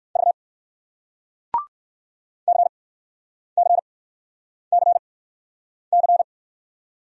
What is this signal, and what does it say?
Morse code, heard as a keyed tone, about 60 wpm, spaced as for 9 wpm, 700 Hz. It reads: V CCCC